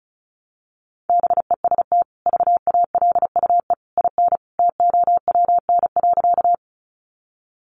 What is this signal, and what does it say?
Morse code, heard as a keyed tone, about 34 wpm, 700 Hz. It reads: BEST VALUE IN TOWN.
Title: Sentences from Top 600 Words 35wpm